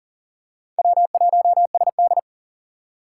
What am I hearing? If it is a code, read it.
Morse code, heard as a keyed tone, about 40 wpm, 700 Hz. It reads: W1SD
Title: USA Call Signs - 6x Character Spacing - 40wpm